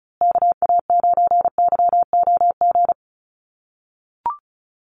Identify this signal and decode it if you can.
Morse code, heard as a keyed tone, about 34 wpm, 700 Hz. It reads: KA9YOG
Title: USA Call Signs Repeated 2x 35wpm